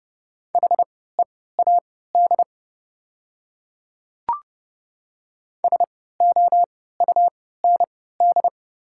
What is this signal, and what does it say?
Morse code, heard as a keyed tone, about 30 wpm, spaced as for 17 wpm, 700 Hz. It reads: HEAD SOUND
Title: 3x Character Spacing - Top 300 Words - 30wpm